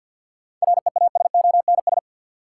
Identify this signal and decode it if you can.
Morse code, heard as a keyed tone, about 51 wpm, 700 Hz. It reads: REASONS
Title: Sets of 4 Words - New Word Drill - 50wpm